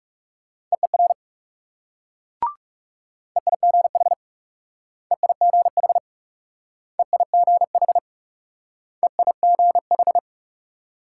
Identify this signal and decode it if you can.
Morse code, heard as a keyed tone, about 37 wpm, 700 Hz. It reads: EER EIGH EIGH EIGH EIGH
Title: Speed-Racing Multi-letter Phonemes - 30wpm